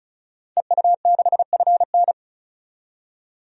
Morse code, heard as a keyed tone, about 36 wpm, 700 Hz. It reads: EU6FN
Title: DXCC-100 Call Signs - 5 Characters - 35wpm